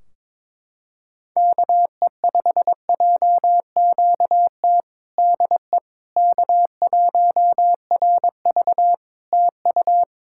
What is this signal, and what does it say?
Morse code, heard as a keyed tone, about 22 wpm, 700 Hz. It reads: KE5JQT DE K1RV TU
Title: Practice QSO - Exchange 4 - Ending - 22wpm